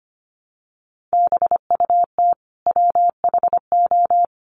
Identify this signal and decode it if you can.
Morse code, heard as a keyed tone, about 25 wpm, 700 Hz. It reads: BUT WHO
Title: Sentences from Top 1100 Words 25wpm